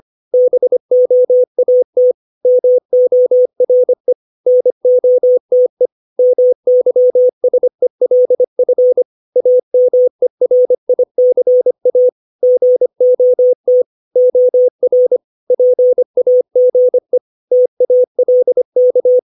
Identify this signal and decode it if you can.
Morse code, heard as a keyed tone, about 25 wpm, 500 Hz. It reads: BOAT MORE NOTE MYSELF AMERICA GOT OR PAGE TALK
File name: 632 words / 632 Words 25WPM